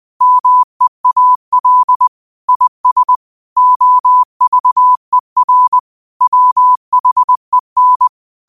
Morse code, heard as a keyed tone, about 20 wpm, 1000 Hz. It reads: MEAL IS OVER WHEN
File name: qotdfiles / 20wpm